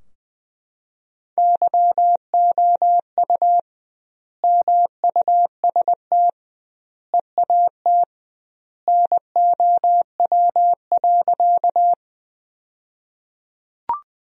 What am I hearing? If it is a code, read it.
Morse code, heard as a keyed tone, about 21 wpm, 700 Hz. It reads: YOU MUST EAT NOW.
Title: 2x Word-Spacing - Sentences from Top 300 Words - 20wpm